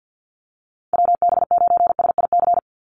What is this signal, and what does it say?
Morse code, heard as a keyed tone, about 49 wpm, 700 Hz. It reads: WB9SIC
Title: USA Call Signs 50wpm